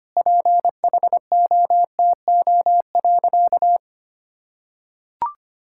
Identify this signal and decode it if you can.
Morse code, heard as a keyed tone, about 25 wpm, 700 Hz. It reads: PHOTO.
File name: practice / 25wpm / Sentences with 4-Word Phrases - 25wpm